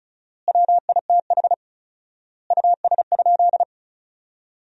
Morse code, heard as a keyed tone, about 34 wpm, 700 Hz. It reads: WITH US?
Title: 4x Word-Spacing - Sentences from Top 200 Words - 35wpm